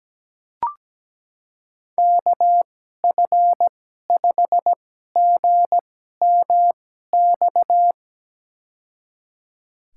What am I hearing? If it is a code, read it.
Morse code, heard as a keyed tone, about 17 wpm, spaced as for 12 wpm, 700 Hz. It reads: KF5GMX
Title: USA Call Signs - 6 Characters - 2x Character Spacing - 17wpm